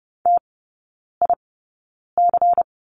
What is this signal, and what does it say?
Morse code, heard as a keyed tone, about 30 wpm, spaced as for 9 wpm, 700 Hz. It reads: TIC
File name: Lesson 16e - Letter c - Three Characters 30wpm